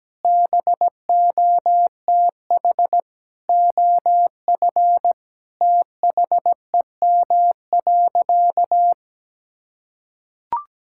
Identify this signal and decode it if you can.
Morse code, heard as a keyed tone, about 17 wpm, 700 Hz. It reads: BOTH OF THEM.
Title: Sentences from Top 1600 Words 17wpm